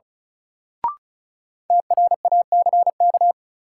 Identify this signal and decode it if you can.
Morse code, heard as a keyed tone, about 36 wpm, 700 Hz. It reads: TRACK
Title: Sentences from Top 1100 Words - New Word Drill - 35wpm